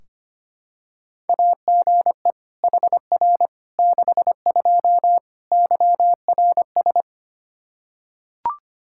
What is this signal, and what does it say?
Morse code, heard as a keyed tone, about 25 wpm, 700 Hz. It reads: AGE HR 62 YRS
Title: Practice QSO - Exchange 3 - More Info - 25wpm